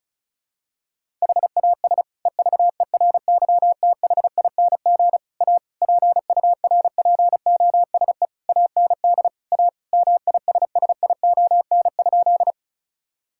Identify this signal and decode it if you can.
Morse code, heard as a keyed tone, about 36 wpm, 700 Hz. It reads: HAS EVERYTHING A PURPOSE AND A MISSION?
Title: Sentences from Top 1600 Words 35wpm